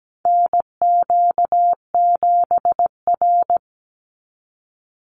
Morse code, heard as a keyed tone, about 17 wpm, 700 Hz. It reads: NQ7R
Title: NA SOTA Chaser Callsigns 17wpm